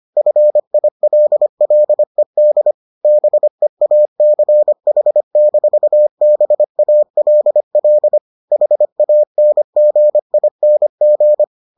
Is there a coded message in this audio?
Yes: Morse code from 0.2 to 11.4 s, heard as a keyed tone, about 25 wpm, 600 Hz. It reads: FILLED BEACH-BALL HANGING